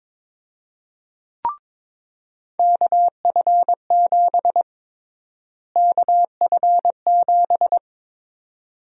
Morse code, heard as a keyed tone, about 22 wpm, 700 Hz. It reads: KF7 KF7
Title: USA Call Sign Prefixes - Repeated 2x - 22wpm